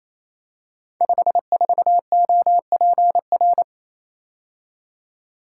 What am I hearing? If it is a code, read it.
Morse code, heard as a keyed tone, about 28 wpm, 700 Hz. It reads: 54OPR
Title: Code Groups - 5 Letter:Numbers - 28wpm